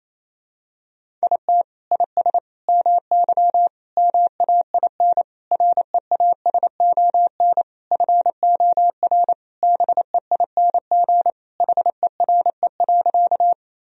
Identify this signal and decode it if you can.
Morse code, heard as a keyed tone, about 28 wpm, 700 Hz. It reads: IT IS MY MAIN REASON FOR BEING HERE.
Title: Sentences from Top 1000 Words - Review Part 2 - 28wpm